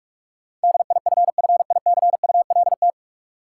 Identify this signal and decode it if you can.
Morse code, heard as a keyed tone, about 47 wpm, 700 Hz. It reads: DIFFICULT
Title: Sets of 4 Words - New Word Drill - 45wpm